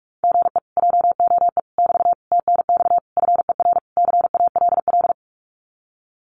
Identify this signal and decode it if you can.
Morse code, heard as a keyed tone, about 46 wpm, 700 Hz. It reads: GE JOE <BT> TNX FER CALL
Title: Common QSO Phrases Encore 45wpm